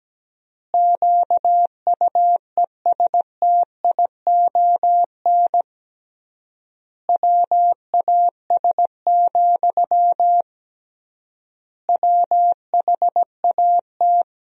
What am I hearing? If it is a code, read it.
Morse code, heard as a keyed tone, about 17 wpm, 700 Hz. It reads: QUESTION WAS, WHAT
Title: 3x Word-Spacing - Sentences from Top 400 Words - 17wpm